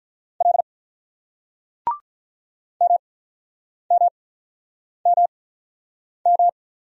Morse code, heard as a keyed tone, about 46 wpm, spaced as for 9 wpm, 700 Hz. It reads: L MMMM